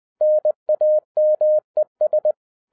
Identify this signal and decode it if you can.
Morse code, heard as a keyed tone, about 20 wpm, 610 Hz. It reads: NAMES